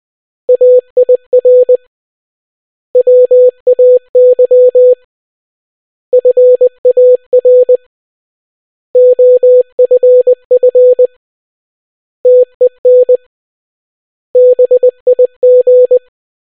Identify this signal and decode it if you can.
Morse code, heard as a keyed tone, about 20 wpm, 500 Hz. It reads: AIR WAY FAR OFF TEN BIG